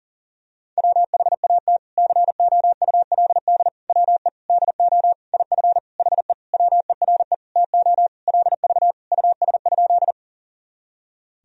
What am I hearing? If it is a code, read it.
Morse code, heard as a keyed tone, about 40 wpm, 700 Hz. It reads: WHAT COULD WE DO IF HE WERE TO LV US?
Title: Sentences from Top 400 Words and Abbreviations 40wpm